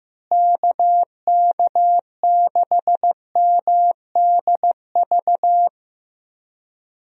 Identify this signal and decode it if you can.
Morse code, heard as a keyed tone, about 15 wpm, 700 Hz. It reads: KK6MDV
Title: USA Call Signs - 6 Characters - Repeated 3x - 15wpm